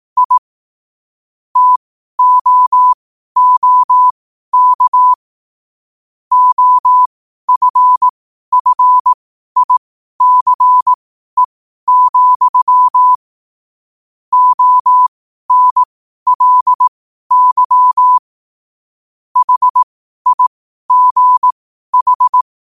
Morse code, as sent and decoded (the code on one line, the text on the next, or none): .. / - --- --- -.- / --- ..-. ..-. .. -.-. --..-- / --- -. .-.. -.-- / .... .. --. ....
I TOOK OFFICE, ONLY HIGH